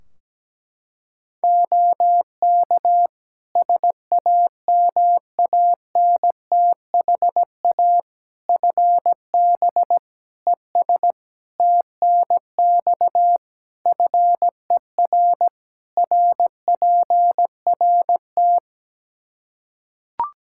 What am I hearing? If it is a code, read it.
Morse code, heard as a keyed tone, about 17 wpm, 700 Hz. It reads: OK SAMANTHA FB ES TNX FER RPRT
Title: Practice QSO - Exchange 2 - Info - 17wpm